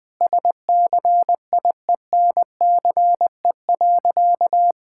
Morse code, heard as a keyed tone, about 21 wpm, 700 Hz. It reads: SCIENCE.
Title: Sentences from Top 1000 Words - Review Part 6 - 20wpm